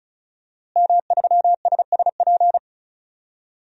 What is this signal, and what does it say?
Morse code, heard as a keyed tone, about 34 wpm, 700 Hz. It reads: M3SSP
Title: DXCC-100 Call Signs - 5 Characters - Repeated 3x - 35wpm